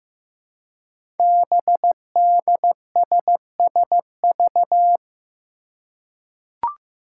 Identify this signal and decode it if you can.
Morse code, heard as a keyed tone, about 15 wpm, 700 Hz. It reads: BDSSV
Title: Five Challenging Letter-Number Combinations 15wpm